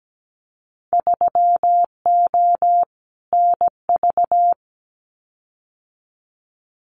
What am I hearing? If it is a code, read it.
Morse code, heard as a keyed tone, about 17 wpm, 700 Hz. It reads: 3O NV
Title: Winter Field Day Exchanges Encore 17wpm